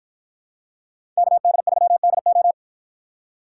Morse code, heard as a keyed tone, about 49 wpm, 700 Hz. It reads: KD3DO